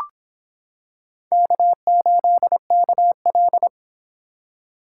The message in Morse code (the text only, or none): K8KL